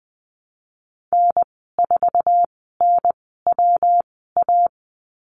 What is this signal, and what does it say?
Morse code, heard as a keyed tone, about 20 wpm, spaced as for 15 wpm, 700 Hz. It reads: N4NWA
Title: 154 Lesson 25g - Letter p - Call Signs 20wpm